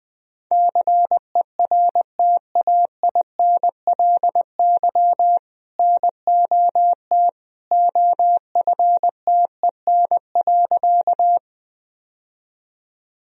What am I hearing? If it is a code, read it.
Morse code, heard as a keyed tone, about 20 wpm, 700 Hz. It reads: CERTAINLY NOT OFTEN.
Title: Sentences from Top 1000 Words - Review Part 1 - 20wpm